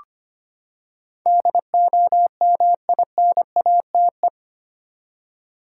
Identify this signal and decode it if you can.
Morse code, heard as a keyed tone, about 25 wpm, 700 Hz. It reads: DOMINATE
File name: Eight Letter Words - 25wpm